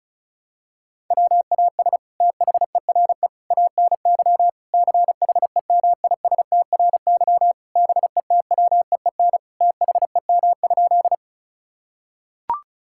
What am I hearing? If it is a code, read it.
Morse code, heard as a keyed tone, about 36 wpm, 700 Hz. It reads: WAS THERE ANY CHEMISTRY BETWEEN THEM?
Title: Sentences from Top 1000 Words - Review Part 6 - 35wpm